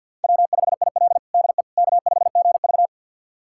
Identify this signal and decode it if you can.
Morse code, heard as a keyed tone, about 48 wpm, 700 Hz. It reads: W5IL DE K5GV
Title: USA Callsign DE Callsign - 4 Characters - 50wpm